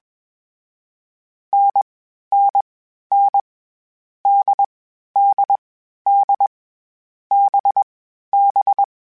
Morse code, heard as a keyed tone, about 20 wpm, 800 Hz. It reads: N N N D D D B B
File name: Difficult Combos / Difficult Combos-mp3s / NDB NDB6 @20